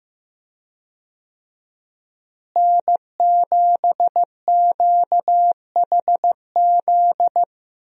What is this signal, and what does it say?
Morse code, heard as a keyed tone, about 15 wpm, 700 Hz. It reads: N7QHZ
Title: USA Call Signs - 5 Characters - Repeated 2x - 15wpm